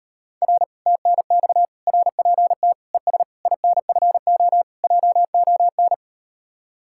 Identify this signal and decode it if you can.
Morse code, heard as a keyed tone, about 39 wpm, 700 Hz. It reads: R TNX RPT ES INFO JON